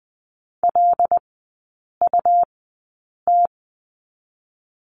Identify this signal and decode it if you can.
Morse code, heard as a keyed tone, about 20 wpm, 700 Hz. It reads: L U T
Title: 107 Lesson 18e - Letter m - Three Characters 20wpm